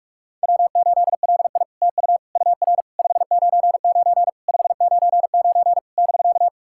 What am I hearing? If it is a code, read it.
Morse code, heard as a keyed tone, about 44 wpm, 700 Hz. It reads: W8LI TU UR 599 599 <BK>
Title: SOTA QSO Practice 45wpm